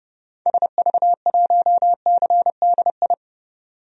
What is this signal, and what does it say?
Morse code, heard as a keyed tone, about 30 wpm, 700 Hz. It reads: SV1CDI
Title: DXCC-100 Call Signs - 6 Characters - 30wpm